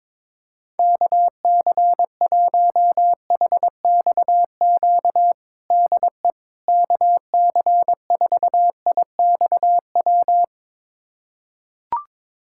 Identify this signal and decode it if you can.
Morse code, heard as a keyed tone, about 22 wpm, 700 Hz. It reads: KC1HXQ DE KC4IXW
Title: USA Callsign DE Callsign - 6 Characters - 22wpm